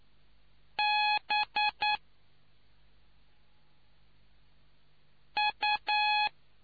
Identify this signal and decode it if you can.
Morse code, heard as a keyed tone, about 9 wpm, 840 Hz. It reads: B U